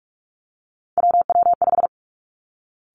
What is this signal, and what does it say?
Morse code, heard as a keyed tone, about 46 wpm, 700 Hz. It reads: WW5
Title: Three Challenging Letter-Number Combinations 45wpm